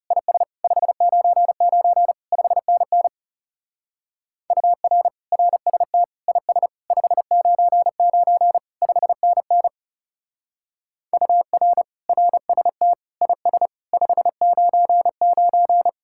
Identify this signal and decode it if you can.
Morse code, heard as a keyed tone, about 40 wpm, 700 Hz. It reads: IS 599 5NN UR RST IS 599 5NN UR RST IS 599